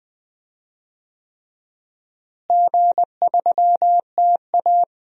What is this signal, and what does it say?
Morse code, heard as a keyed tone, about 21 wpm, 700 Hz. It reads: G3TA
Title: DXCC-100 Call Signs - 4 Characters - Repeated 3x - 20wpm